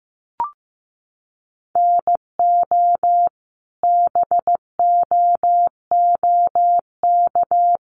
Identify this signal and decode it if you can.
Morse code, heard as a keyed tone, about 15 wpm, 700 Hz. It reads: NO BOOK
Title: Sentences from Top 1200 Words 15wpm